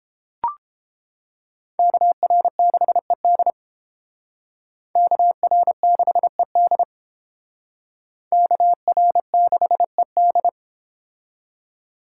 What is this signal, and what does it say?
Morse code, heard as a keyed tone, about 30 wpm, 700 Hz. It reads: KR6ED KR6ED KR6ED